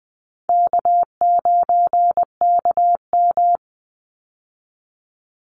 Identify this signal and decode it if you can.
Morse code, heard as a keyed tone, about 20 wpm, 700 Hz. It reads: K9KM
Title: Call Signs Repeated 2x - 20wpm